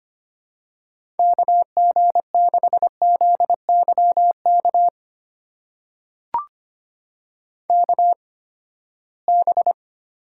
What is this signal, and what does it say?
Morse code, heard as a keyed tone, about 25 wpm, spaced as for 17 wpm, 700 Hz. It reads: KG6ZYK KB
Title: USA Call Signs - 6 Characters - 8x Character Spacing - 25wpm